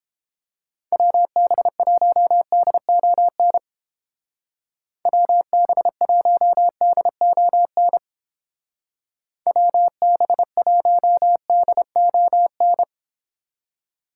Morse code, heard as a keyed tone, about 30 wpm, 700 Hz. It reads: WB1DON WB1DON WB1DON